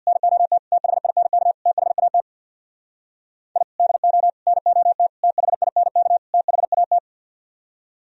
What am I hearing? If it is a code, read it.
Morse code, heard as a keyed tone, about 64 wpm, 700 Hz. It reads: NOT THINK THAT I DO NOT THINK THAT